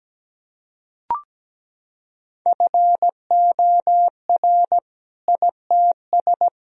Morse code, heard as a keyed tone, about 17 wpm, 700 Hz. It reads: FOR ITS
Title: Sentences from Top 1000 Words - Review Part 3 - 17wpm